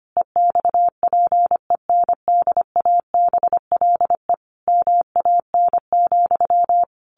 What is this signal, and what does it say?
Morse code, heard as a keyed tone, about 25 wpm, 700 Hz. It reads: EXPENDABLE MAN,
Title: A Sound Of Thunder 25wpm